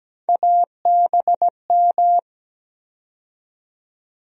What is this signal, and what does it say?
Morse code, heard as a keyed tone, about 17 wpm, 700 Hz. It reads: ABM
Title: USA Call Sign Postfixes - 3 Letters 17wpm